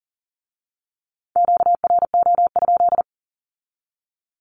Morse code, heard as a keyed tone, about 41 wpm, 700 Hz. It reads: QRO?
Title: Q-Codes with Spoken Meaning - 40wpm